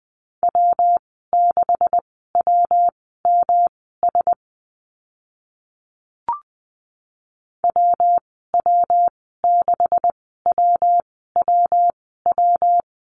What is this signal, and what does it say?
Morse code, heard as a keyed tone, about 20 wpm, spaced as for 14 wpm, 700 Hz. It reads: W6WMS WW6WWW
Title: 140 Lesson 23g - Letter f - Call Signs 20wpm